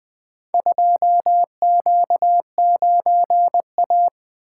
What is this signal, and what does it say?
Morse code, heard as a keyed tone, about 20 wpm, 700 Hz. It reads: 2Q9A
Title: Four Letter-Number Combinations - 20wpm